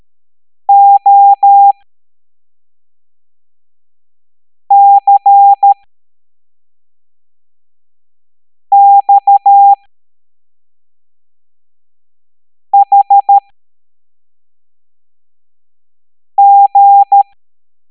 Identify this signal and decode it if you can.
Morse code, heard as a keyed tone, about 13 wpm, spaced as for 3 wpm, 800 Hz. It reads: OCXHG